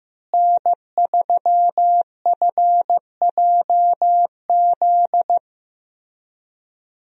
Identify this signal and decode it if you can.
Morse code, heard as a keyed tone, about 16 wpm, 700 Hz. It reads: N3FJZ
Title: USA Call Signs - 5 Characters - Repeated 2x - 15wpm